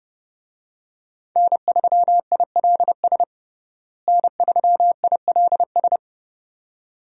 Morse code, heard as a keyed tone, about 32 wpm, 700 Hz. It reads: N3ILS N3ILS
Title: USA Call Signs - 5 Characters - Repeated 2x - 30wpm